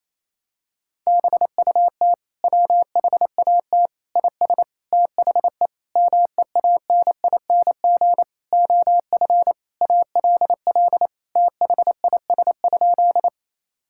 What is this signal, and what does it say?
Morse code, heard as a keyed tone, about 27 wpm, 700 Hz. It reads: BUT WHAT IS THE MEANING OF ALL THIS?
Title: Sentences from Top 1000 Words - Review Part 6 - 28wpm